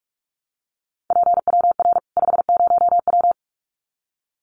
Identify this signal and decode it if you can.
Morse code, heard as a keyed tone, about 44 wpm, 700 Hz. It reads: PWR 50W
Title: Common QSO Phrases 45wpm